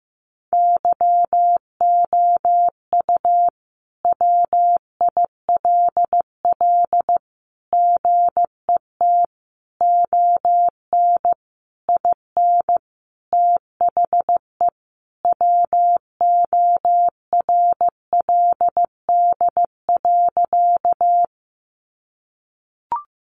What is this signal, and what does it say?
Morse code, heard as a keyed tone, about 15 wpm, 700 Hz. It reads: YOU WILL GET ON IN THE WORLD.
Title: Sentences from Top 200 Words 15wpm